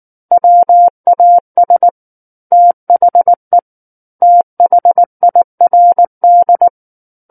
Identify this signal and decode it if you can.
Morse code, heard as a keyed tone, about 19 wpm, 700 Hz. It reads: WAS THE THIRD
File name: Book One / 10-Chapter 10 - 19 WPM